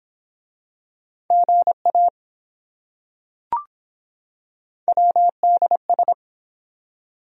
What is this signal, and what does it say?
Morse code, heard as a keyed tone, about 24 wpm, 700 Hz. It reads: GA WDS